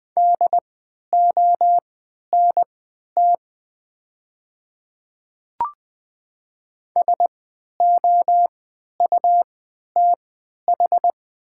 Morse code, heard as a keyed tone, about 20 wpm, spaced as for 11 wpm, 700 Hz. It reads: DONT SOUTH
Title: 3x Character Spacing - Top 500 Words - 20wpm